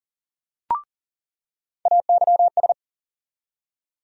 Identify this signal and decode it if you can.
Morse code, heard as a keyed tone, about 40 wpm, 700 Hz. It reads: AYS
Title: USA Call Sign Postfixes - 3 Letters 40wpm